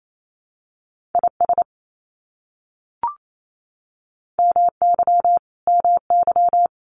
Morse code, heard as a keyed tone, about 28 wpm, 700 Hz. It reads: IS MY MY
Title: Top 500 Words Repeated 28wpm